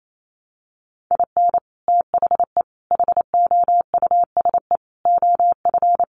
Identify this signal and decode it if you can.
Morse code, heard as a keyed tone, about 28 wpm, 700 Hz. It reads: IN THE HOUSE OF
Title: Sets of 5 Words Encore 28wpm